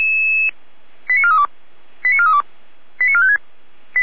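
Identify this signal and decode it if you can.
Five-tone selective call (ZVEI), heard as quick runs of five tones, tones of about 70 ms. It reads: 89432 89432 89456
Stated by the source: ZVEI five-tone call received on SDR.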